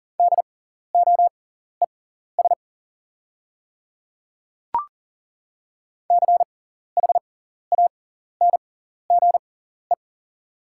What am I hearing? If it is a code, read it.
Morse code, heard as a keyed tone, about 40 wpm, spaced as for 14 wpm, 700 Hz. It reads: DOES CHANGE
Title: 6x Character Spacing - Top 200 Words - 40wpm